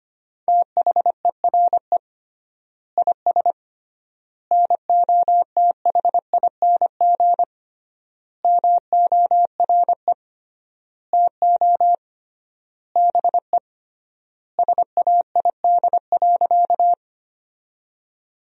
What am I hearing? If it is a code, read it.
Morse code, heard as a keyed tone, about 25 wpm, 700 Hz. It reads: THERE IS NOTHING MORE TO BE SAID.
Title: 3x Word-Spacing - Sentences from Top 500 Words - 25wpm